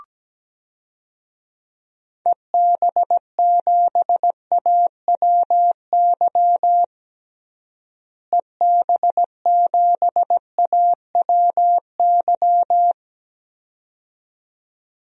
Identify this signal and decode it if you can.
Morse code, heard as a keyed tone, about 17 wpm, 700 Hz. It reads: EB7AWY EB7AWY
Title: DXCC-100 Call Signs - 6 Characters - Repeated 2x - 17wpm